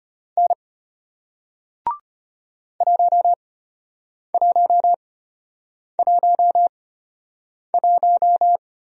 Morse code, heard as a keyed tone, about 33 wpm, spaced as for 8 wpm, 700 Hz. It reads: N 1111